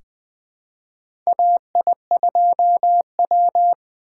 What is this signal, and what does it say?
Morse code, heard as a keyed tone, about 20 wpm, 700 Hz. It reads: AI2W
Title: USA Call Signs - 4 Characters - 3x Character Spacing - 20wpm